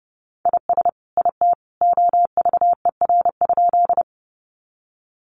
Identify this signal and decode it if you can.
Morse code, heard as a keyed tone, about 30 wpm, 700 Hz. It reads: IS IT OVER?
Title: Sentences from Top 100 Words 30wpm